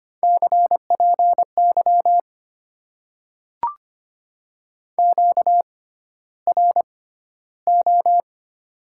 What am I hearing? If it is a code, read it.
Morse code, heard as a keyed tone, about 25 wpm, spaced as for 13 wpm, 700 Hz. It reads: CPY QRO
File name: Top 90 QSO Elements - 6x Character Spacing - 25wpm